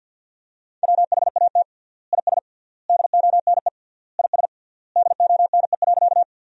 Morse code, heard as a keyed tone, about 50 wpm, 700 Hz. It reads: WHAT IS DONE IS DONE.
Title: 3x Word-Spacing - Sentences from Top 500 Words - 50wpm